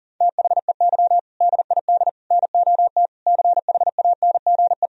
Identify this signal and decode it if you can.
Morse code, heard as a keyed tone, about 40 wpm, 700 Hz. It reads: THEY DID NOT CHANGE
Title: Sentences from Top 1000 Words - Review Part 1 - 40wpm